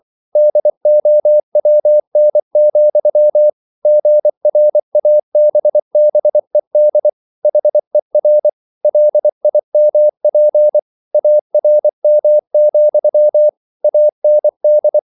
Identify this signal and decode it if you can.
Morse code, heard as a keyed tone, about 24 wpm, 600 Hz. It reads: DOWN, GRABBED HER LIMP ARM, AND